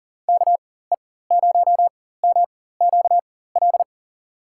Morse code, heard as a keyed tone, about 40 wpm, spaced as for 19 wpm, 700 Hz. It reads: KE0MQL
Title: USA Call Signs - 6 Characters - 4x Character Spacing - 40wpm